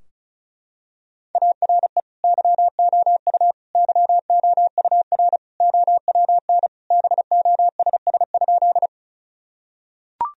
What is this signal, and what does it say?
Morse code, heard as a keyed tone, about 36 wpm, 700 Hz. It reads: ARE YOU YOUR OWN BOSS?
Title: Sentences from Top 1000 Words - Review Part 4 - 35wpm